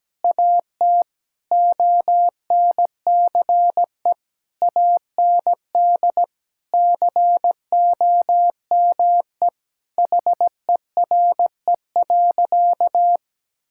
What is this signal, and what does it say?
Morse code, heard as a keyed tone, about 17 wpm, 700 Hz. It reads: AT ONCE AND COME HERE.
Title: Sentences from Top 1600 Words 17wpm